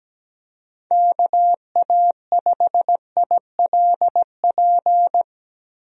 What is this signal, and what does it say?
Morse code, heard as a keyed tone, about 17 wpm, 700 Hz. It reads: KA5ILP
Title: USA Call Signs - 6 Characters - Repeated 2x - 17wpm